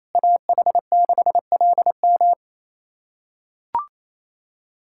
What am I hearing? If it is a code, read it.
Morse code, heard as a keyed tone, about 28 wpm, 700 Hz. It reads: AH6LM
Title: USA Callsign DE Callsign - 5 Characters - 28wpm